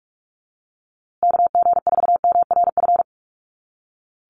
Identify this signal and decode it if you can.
Morse code, heard as a keyed tone, about 44 wpm, 700 Hz. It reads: KG4MRF
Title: USA Call Signs Repeated 2x 45wpm